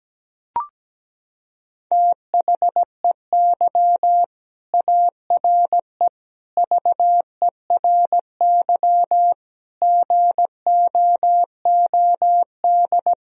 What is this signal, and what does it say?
Morse code, heard as a keyed tone, about 17 wpm, 700 Hz. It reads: THEY ARE VERY GOOD
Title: Sentences from Top 1000 Words - Review Part 3 - 17wpm